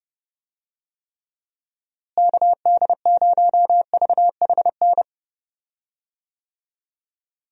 Morse code, heard as a keyed tone, about 32 wpm, 700 Hz. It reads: KD0VHN
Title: USA Call Signs - 6 Characters - Repeated 2x - 30wpm